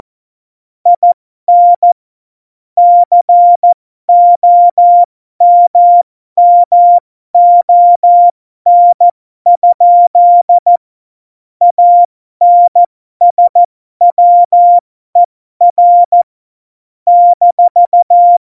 Morse code, heard as a keyed tone, about 14 wpm, spaced as for 12 wpm, 700 Hz. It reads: IN COMMON? ANSWER -
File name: Morsekurs / Witze